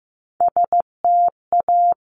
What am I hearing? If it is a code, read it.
Morse code, heard as a keyed tone, about 15 wpm, 700 Hz. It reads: STA
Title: Top Three Letter Combinations in Words 15wpm